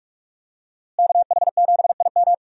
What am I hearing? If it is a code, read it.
Morse code, heard as a keyed tone, about 46 wpm, 700 Hz. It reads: KH7IM
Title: USA Call Signs - 5 Characters - Repeated 3x - 45wpm